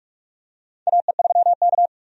Morse code, heard as a keyed tone, about 46 wpm, 700 Hz. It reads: AE3K
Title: USA Call Signs - 4 Characters - 45wpm